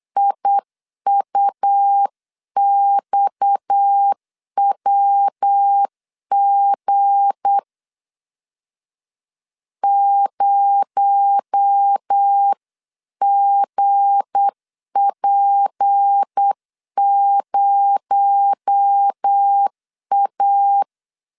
Morse code, heard as a keyed tone, about 8 wpm, 800 Hz. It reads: IUXWG 0GP0A
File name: LEZIONI / 7 GF9Z / 7 G 8wpm 3118